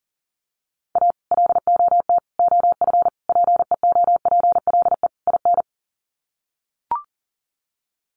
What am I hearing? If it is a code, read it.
Morse code, heard as a keyed tone, about 40 wpm, 700 Hz. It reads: A LOT OF PEOPLE IN
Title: Sets of 5 Words 40wpm